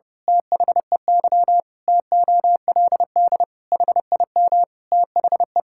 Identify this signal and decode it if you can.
Morse code, heard as a keyed tone, about 32 wpm, 700 Hz. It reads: THEY TOLD HIM THE
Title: Pangram Challenge v2 - 30wpm